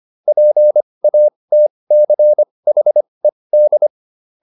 Morse code, heard as a keyed tone, about 25 wpm, spaced as for 20 wpm, 600 Hz. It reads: PATCHED